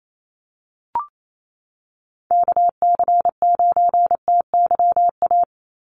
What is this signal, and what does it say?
Morse code, heard as a keyed tone, about 28 wpm, 700 Hz. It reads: KC9TYA
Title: NA SOTA Chaser Callsigns 28wpm